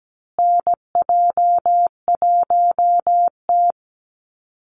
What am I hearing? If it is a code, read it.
Morse code, heard as a keyed tone, about 17 wpm, 700 Hz. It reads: NJ1T
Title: Call Signs Repeated 3x - 17wpm